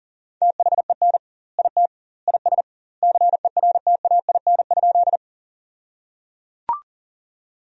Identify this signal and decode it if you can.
Morse code, heard as a keyed tone, about 37 wpm, 700 Hz. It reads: THEN IT IS CERTAIN?
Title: 2x Word-Spacing - Sentences from Top 500 Words - 40wpm